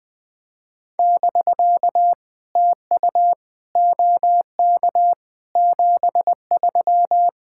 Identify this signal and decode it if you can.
Morse code, heard as a keyed tone, about 20 wpm, 660 Hz. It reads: <BK> TU OK 73
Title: POTA QSO Practice - 20wpm